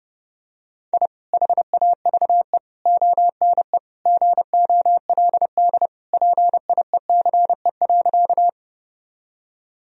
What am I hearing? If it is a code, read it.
Morse code, heard as a keyed tone, about 30 wpm, 700 Hz. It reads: I HAVE ONE GOLD PIECE.
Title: Sentences from Top 1000 Words - Review Part 3 - 30wpm